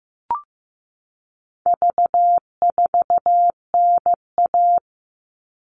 Morse code, heard as a keyed tone, about 15 wpm, 700 Hz. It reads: V4NA
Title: All Cut Numbers - 4-Digit 15wpm